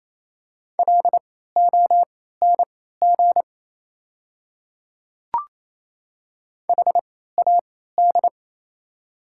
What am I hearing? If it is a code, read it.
Morse code, heard as a keyed tone, about 28 wpm, spaced as for 16 wpm, 700 Hz. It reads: LONG HAD
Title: 3x Character Spacing - Top 100 Words - 28wpm